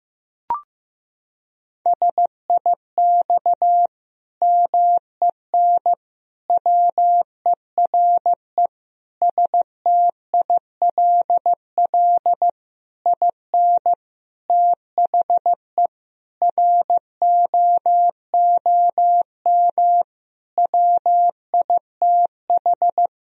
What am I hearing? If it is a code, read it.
Morse code, heard as a keyed tone, about 15 wpm, 700 Hz. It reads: SIX MEN WERE STILL IN THE ROOM WITH
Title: Sentences from Top 1000 Words - Review Part 2 - 15wpm